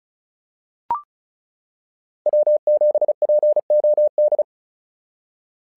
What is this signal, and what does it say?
Morse code, heard as a keyed tone, about 35 wpm, 600 Hz. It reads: W7POD
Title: SOTA QSO Practice 35wpm